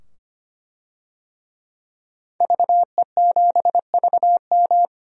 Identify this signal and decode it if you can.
Morse code, heard as a keyed tone, about 25 wpm, 700 Hz. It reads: VE7VM
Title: DXCC-100 Call Signs - 5 Characters - 25wpm